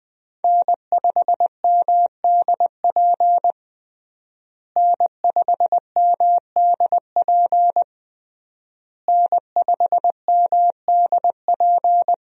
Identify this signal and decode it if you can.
Morse code, heard as a keyed tone, about 20 wpm, 700 Hz. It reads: N5MDP N5MDP N5MDP
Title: USA Call Signs - 5 Characters - Repeated 3x - 20wpm